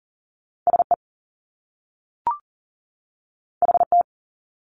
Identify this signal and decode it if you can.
Morse code, heard as a keyed tone, about 40 wpm, 700 Hz. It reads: SE HT
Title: Top Two Letter Combinations in Words 40wpm